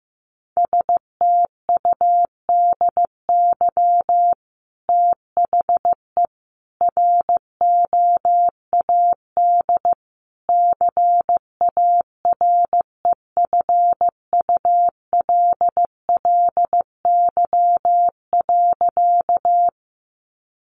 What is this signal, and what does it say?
Morse code, heard as a keyed tone, about 15 wpm, 700 Hz. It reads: STUDY THE ROAD CAREFULLY.
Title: Sentences from Top 1300 Words 15wpm